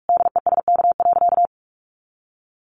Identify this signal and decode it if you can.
Morse code, heard as a keyed tone, about 44 wpm, 700 Hz. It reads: DESK.
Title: A Sound Of Thunder 45wpm